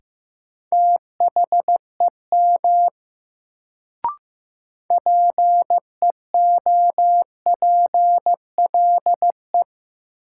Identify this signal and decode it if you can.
Morse code, heard as a keyed tone, about 15 wpm, 700 Hz. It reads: THEM PEOPLE
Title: Sets of 2 Words - New Word Drill - 15wpm